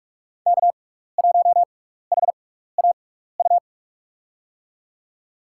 Morse code, heard as a keyed tone, about 45 wpm, spaced as for 15 wpm, 700 Hz. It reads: K1HAU
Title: USA Call Signs - 5 Characters - 6x Character Spacing - 45wpm